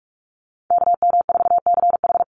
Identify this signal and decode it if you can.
Morse code, heard as a keyed tone, about 44 wpm, 700 Hz. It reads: KM4CH
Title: Call Signs Repeated 2x - 45wpm